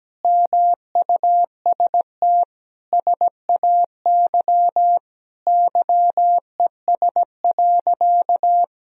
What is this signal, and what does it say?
Morse code, heard as a keyed tone, about 17 wpm, 700 Hz. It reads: MUST SAY YES.
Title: Sentences from Top 500 Words and Abbreviations 17wpm